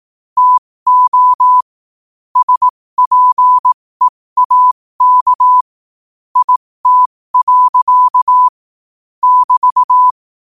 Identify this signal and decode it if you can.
Morse code, heard as a keyed tone, about 18 wpm, spaced as for 15 wpm, 1000 Hz. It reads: TO SPEAK IT. <BT>
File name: qotdfiles / 15wpm